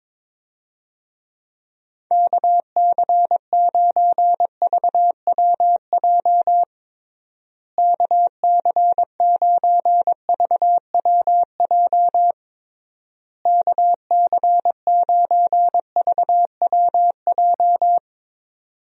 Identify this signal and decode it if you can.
Morse code, heard as a keyed tone, about 22 wpm, 700 Hz. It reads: KC9VWJ KC9VWJ KC9VWJ
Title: USA Call Signs - 6 Characters - Repeated 3x - 22wpm